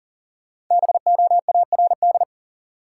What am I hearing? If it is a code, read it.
Morse code, heard as a keyed tone, about 40 wpm, 700 Hz. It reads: BOARD
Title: Sentences from Top 700 Words - New Word Drill - 40wpm